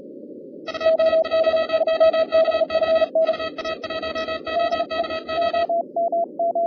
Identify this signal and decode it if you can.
Morse code, heard as a keyed tone, about 38 wpm, 3200 Hz, with noise. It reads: UA1OKO UA1OKO
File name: 2011 bcc-challenge samstag tape